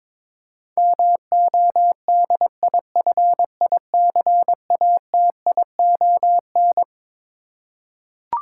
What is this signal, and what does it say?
Morse code, heard as a keyed tone, about 22 wpm, 700 Hz. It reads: MODIFICATION
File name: Twelve Letter Words - 22wpm